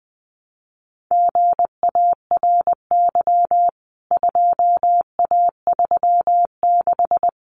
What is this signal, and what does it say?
Morse code, heard as a keyed tone, about 20 wpm, 700 Hz. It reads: GARY 2A36
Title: CWT Exchange 20wpm